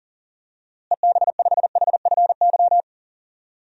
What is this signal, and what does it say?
Morse code, heard as a keyed tone, about 39 wpm, 700 Hz. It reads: EB5HFY
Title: DXCC-100 Call Signs - 6 Characters - Repeated 2x - 40wpm